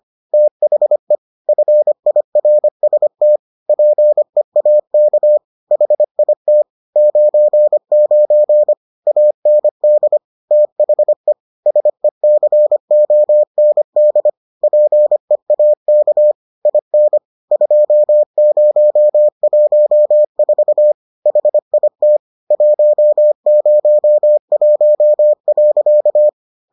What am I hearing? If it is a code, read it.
Morse code, heard as a keyed tone, about 25 wpm, 600 Hz. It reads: THE FIRST PEAK HIT 99 AND THE SECOND PEAK IN 2014 HIT 101.